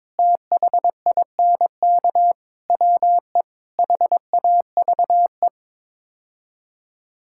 Morse code, heard as a keyed tone, about 22 wpm, 700 Hz. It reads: THINK WE HAVE
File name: practice / 22wpm / Build A Phrase - 5-Word Phrases - 22wpm